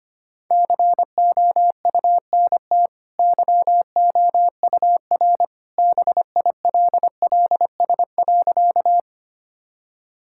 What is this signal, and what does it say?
Morse code, heard as a keyed tone, about 25 wpm, 700 Hz. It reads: COUNT YOUR BILLS.
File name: Easy Sentences with Hints - 25wpm